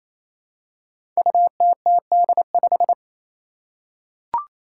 Morse code, heard as a keyed tone, about 28 wpm, 700 Hz. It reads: UTTD5
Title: Five Challenging Letter-Number Combinations 28wpm